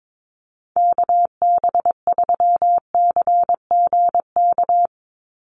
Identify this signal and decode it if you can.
Morse code, heard as a keyed tone, about 22 wpm, 700 Hz. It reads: KB3CGK
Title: USA Call Signs 22wpm